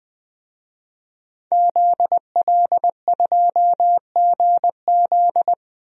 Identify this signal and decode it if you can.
Morse code, heard as a keyed tone, about 20 wpm, 700 Hz. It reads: ZL2GZ
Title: DXCC-100 Call Signs - 5 Characters - Repeated 3x - 20wpm